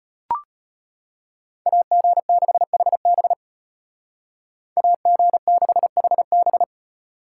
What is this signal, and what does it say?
Morse code, heard as a keyed tone, about 33 wpm, 700 Hz. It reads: AG6HB AG6HB